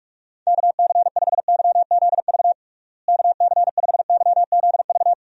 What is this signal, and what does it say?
Morse code, heard as a keyed tone, about 46 wpm, 700 Hz. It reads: KC5YZV KC5YZV
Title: USA Call Signs - 6 Characters - Repeated 3x - 45wpm